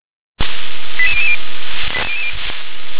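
Five-tone selective call (ZVEI), heard as quick runs of five tones, tones of about 70 ms. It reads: 88000
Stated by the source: ZVEI five-tone selective call received on SDR.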